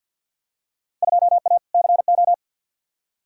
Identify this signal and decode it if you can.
Morse code, heard as a keyed tone, about 49 wpm, 700 Hz. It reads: 2A CO
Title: Field Day Exchanges - 50wpm